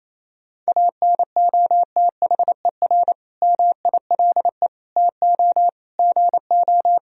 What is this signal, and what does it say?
Morse code, heard as a keyed tone, about 28 wpm, 700 Hz. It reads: ANOTHER MILE TO GO
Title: Sentences from Top 1000 Words - Review Part 1 - 28wpm